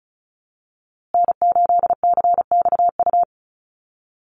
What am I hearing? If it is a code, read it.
Morse code, heard as a keyed tone, about 34 wpm, 700 Hz. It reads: N8CXU
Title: USA Call Signs Repeated 2x 35wpm